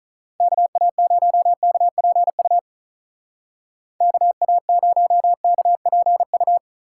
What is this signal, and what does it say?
Morse code, heard as a keyed tone, about 46 wpm, 700 Hz. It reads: KA0KPU KA0KPU